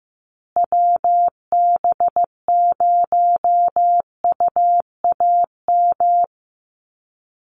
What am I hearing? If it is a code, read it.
Morse code, heard as a keyed tone, about 15 wpm, 700 Hz. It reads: WB0UAM
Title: USA Call Signs Repeated 2x 15wpm